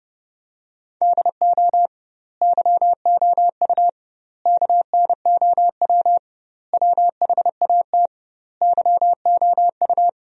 Morse code, heard as a keyed tone, about 30 wpm, 700 Hz. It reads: DO YOU KNOW WHAT YOU
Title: 2x Word-Spacing - Sentences from Top 100 Words - 30wpm